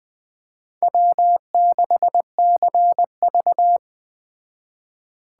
Morse code, heard as a keyed tone, about 21 wpm, 700 Hz. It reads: W6CV